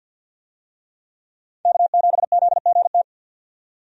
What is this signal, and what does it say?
Morse code, heard as a keyed tone, about 52 wpm, 700 Hz. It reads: K7ZGT
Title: USA Call Signs - 5 Characters - 50wpm